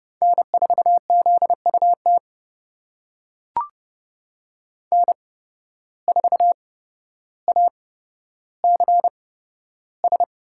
Morse code, heard as a keyed tone, about 30 wpm, spaced as for 13 wpm, 700 Hz. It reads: N4ZUT N4ACS
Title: USA Call Signs - 5 Characters - 8x Character Spacing - 30wpm